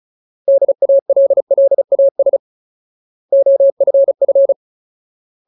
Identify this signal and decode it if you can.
Morse code, heard as a keyed tone, about 34 wpm, 550 Hz. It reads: DALLAS OFF
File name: Ultimate Word List / 35WPM Ultimate